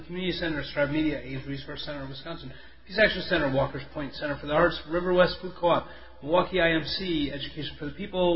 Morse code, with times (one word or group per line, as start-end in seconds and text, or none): none